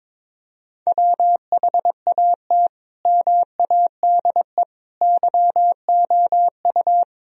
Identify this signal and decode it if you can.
Morse code, heard as a keyed tone, about 22 wpm, 700 Hz. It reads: WHAT MADE YOU
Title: Sentences from Top 1000 Words - Review Part 4 - 22wpm